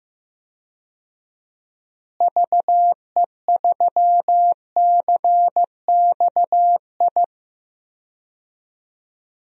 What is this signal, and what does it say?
Morse code, heard as a keyed tone, about 15 wpm, 700 Hz. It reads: VE3CXI
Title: DXCC-100 Call Signs - 6 Characters - Repeated 3x - 15wpm